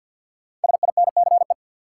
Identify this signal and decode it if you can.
Morse code, heard as a keyed tone, about 53 wpm, 700 Hz. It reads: SINCE
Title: Sets of 2 Words - New Word Drill - 50wpm